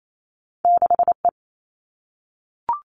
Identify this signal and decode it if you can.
Morse code, heard as a keyed tone, about 28 wpm, 700 Hz. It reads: 6E
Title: All Cut Numbers - 2-Digit 28wpm